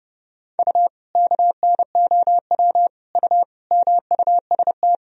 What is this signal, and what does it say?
Morse code, heard as a keyed tone, about 30 wpm, 700 Hz. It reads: U KNOW U MUST
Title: Sentences from Top 500 Words and Abbreviations 30wpm